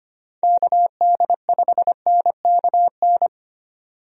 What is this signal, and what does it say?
Morse code, heard as a keyed tone, about 25 wpm, 700 Hz. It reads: KD5NKN
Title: USA Call Signs - 6 Characters - 25wpm